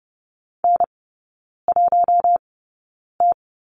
Morse code, heard as a keyed tone, about 30 wpm, spaced as for 9 wpm, 700 Hz. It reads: N1T
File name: 081 Lesson 14e - Number 2 - Three Characters 30wpm